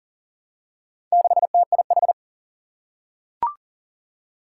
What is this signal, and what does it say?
Morse code, heard as a keyed tone, about 39 wpm, 700 Hz. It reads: 6TIH
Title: Four Challenging Letter-Number Combinations 40wpm